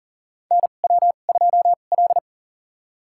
Morse code, heard as a keyed tone, about 40 wpm, spaced as for 29 wpm, 700 Hz. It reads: NW2L
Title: USA Call Signs - 4 Characters - 2x Character Spacing - 40wpm